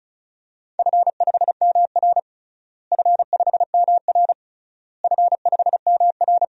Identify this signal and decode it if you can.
Morse code, heard as a keyed tone, about 34 wpm, 700 Hz. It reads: F5MR F5MR F5MR
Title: DXCC-100 Call Signs - 4 Characters - Repeated 3x - 35wpm